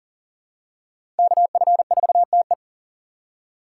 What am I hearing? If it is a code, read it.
Morse code, heard as a keyed tone, about 40 wpm, 700 Hz. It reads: KF4TE
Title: USA Call Signs - 5 Characters - 4x Character Spacing - 40wpm